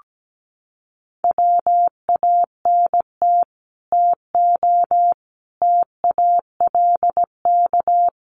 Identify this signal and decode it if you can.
Morse code, heard as a keyed tone, about 17 wpm, 700 Hz. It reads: WANT TO TALK
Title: Sets of 5 Words Encore 17wpm